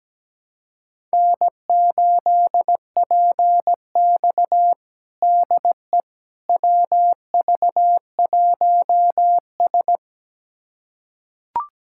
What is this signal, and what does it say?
Morse code, heard as a keyed tone, about 17 wpm, 700 Hz. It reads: N8PX DE WV1S
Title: USA Callsign DE Callsign - 4 Characters - 17wpm